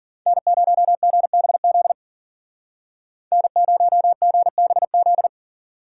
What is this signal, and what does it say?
Morse code, heard as a keyed tone, about 53 wpm, 700 Hz. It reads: N0GBZ N0GBZ